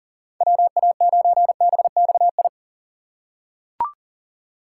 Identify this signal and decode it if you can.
Morse code, heard as a keyed tone, about 37 wpm, 700 Hz. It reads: WA9BXI
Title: USA Call Signs - 6 Characters - 4x Character Spacing - 40wpm